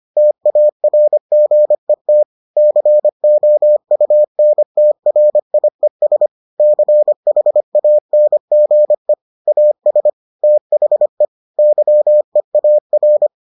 Morse code, heard as a keyed tone, about 25 wpm, 600 Hz. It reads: TARGET COUNTRIES CHANGE AS THE YEAR